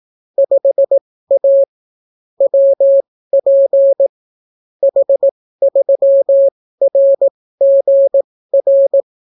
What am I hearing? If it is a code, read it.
Morse code, heard as a keyed tone, about 18 wpm, spaced as for 14 wpm, 550 Hz. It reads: 5A WP H3RGR